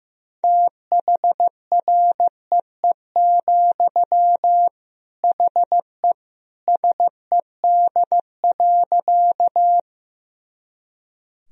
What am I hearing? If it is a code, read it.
Morse code, heard as a keyed tone, about 15 wpm, 700 Hz. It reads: THREE, HE SED.
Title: Sentences from Top 200 Words and Abbreviations 15wpm